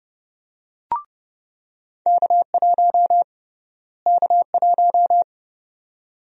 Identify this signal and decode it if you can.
Morse code, heard as a keyed tone, about 30 wpm, 700 Hz. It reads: K1 K1
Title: USA Call Sign Prefixes - Repeated 2x - 30wpm